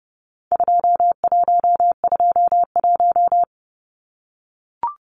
Morse code, heard as a keyed tone, about 30 wpm, 700 Hz. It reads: 2121
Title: Numbers - 4-Digit 30wpm